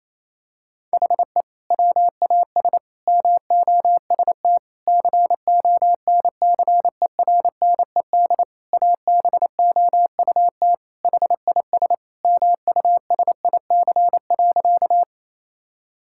Morse code, heard as a keyed tone, about 28 wpm, 700 Hz. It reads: HE WAS MOST CONCERNED ABOUT HIS MUSIC.
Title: Sentences from Top 1000 Words - Review Part 6 - 28wpm